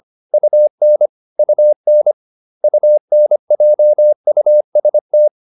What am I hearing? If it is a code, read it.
Morse code, heard as a keyed tone, about 25 wpm, 600 Hz. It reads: UN UN UNJUST